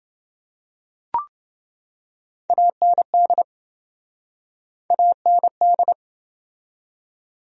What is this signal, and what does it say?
Morse code, heard as a keyed tone, about 28 wpm, 700 Hz. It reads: AND AND